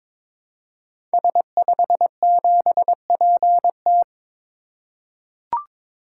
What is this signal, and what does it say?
Morse code, heard as a keyed tone, about 22 wpm, 700 Hz. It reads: S57PT
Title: Code Groups - 5 Letter:Numbers - 22wpm